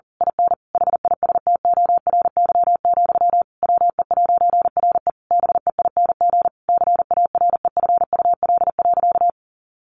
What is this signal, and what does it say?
Morse code, heard as a keyed tone, about 40 wpm, 700 Hz. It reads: IN HISTORY, WE'RE BEING CAREFUL.
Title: A Sound Of Thunder 40wpm